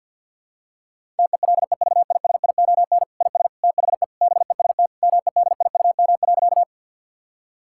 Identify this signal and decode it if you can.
Morse code, heard as a keyed tone, about 50 wpm, 700 Hz. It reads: TELEVISION IS THE BEST MEDIUM.
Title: Sentences from Top 1000 Words - Review Part 6 - 50wpm